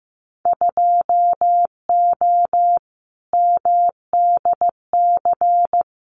Morse code, heard as a keyed tone, about 15 wpm, 700 Hz. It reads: 2O MDC
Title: Winter Field Day Exchanges 15wpm